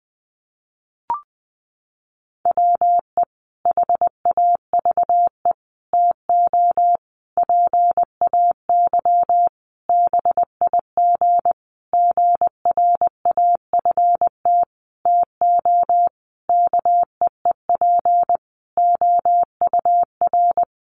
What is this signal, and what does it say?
Morse code, heard as a keyed tone, about 20 wpm, 700 Hz. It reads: WE HAVE TO PAY BIG GRAFT TO KEEP OUR
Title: A Sound Of Thunder 20wpm